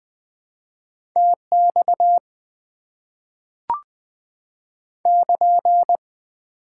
Morse code, heard as a keyed tone, about 20 wpm, 690 Hz. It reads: TX <KN>
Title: Top 90 QSO Elements - 4x Character Spacing - 20wpm